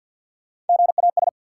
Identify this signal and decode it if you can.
Morse code, heard as a keyed tone, about 53 wpm, 700 Hz. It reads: GAS
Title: Sentences from Top 1100 Words - New Word Drill Alt - 50wpm